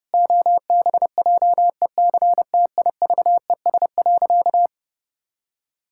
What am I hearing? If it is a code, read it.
Morse code, heard as a keyed tone, about 30 wpm, 700 Hz. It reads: OBJECTIVES.
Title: Pangram Challenge v2 - 30wpm